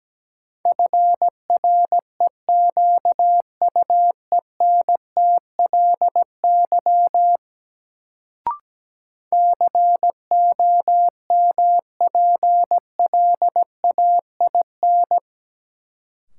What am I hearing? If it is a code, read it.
Morse code, heard as a keyed tone, about 17 wpm, 700 Hz. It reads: FREQUENTLY COMPLAIN
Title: Sentences from Top 1300 Words - New Word Drill Alt - 17wpm